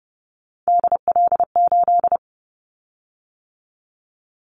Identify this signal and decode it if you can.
Morse code, heard as a keyed tone, about 30 wpm, 700 Hz. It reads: DL8
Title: Callsign Prefixes for Top 25 DXCC - 30wpm